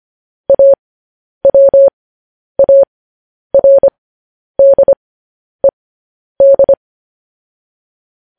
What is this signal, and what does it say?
Morse code, heard as a keyed tone, about 25 wpm, spaced as for 10 wpm, 550 Hz. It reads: AWARDED
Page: news-headlines-in-morse-code-at-10-wpm